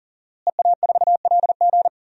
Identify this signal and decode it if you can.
Morse code, heard as a keyed tone, about 37 wpm, 700 Hz. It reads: EA4LG
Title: DXCC-100 Call Signs - 5 Characters - Repeated 3x - 40wpm